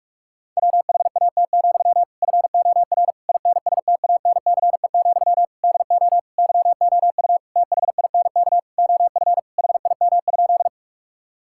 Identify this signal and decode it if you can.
Morse code, heard as a keyed tone, about 46 wpm, 700 Hz. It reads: WHAT, FOR INSTANCE, DO YOU THINK OF HIM?
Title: Sentences from Top 1000 Words - Review Part 5 - 45wpm